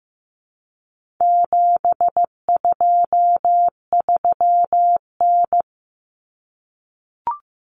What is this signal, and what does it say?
Morse code, heard as a keyed tone, about 15 wpm, 700 Hz. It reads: 723N
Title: Common Cut Numbers - 4-Digit 15wpm